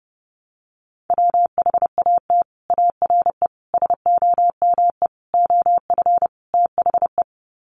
Morse code, heard as a keyed tone, about 30 wpm, 700 Hz. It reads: WHAT ARE SOME OF THE
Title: Sets of 5 Words Encore 30wpm